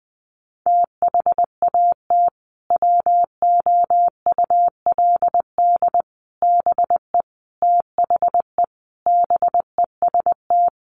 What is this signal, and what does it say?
Morse code, heard as a keyed tone, about 20 wpm, 700 Hz. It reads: THAT WOULD BE THE BEST
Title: Sentences from Top 400 Words 20wpm